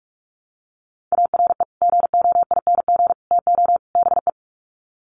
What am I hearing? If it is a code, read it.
Morse code, heard as a keyed tone, about 44 wpm, 700 Hz. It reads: ARE GOING TO BE
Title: Sets of 4 Words 45wpm